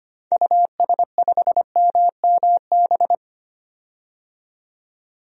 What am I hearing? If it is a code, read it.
Morse code, heard as a keyed tone, about 25 wpm, 700 Hz. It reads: US5MMB
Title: DXCC-100 Call Signs - 6 Characters - 25wpm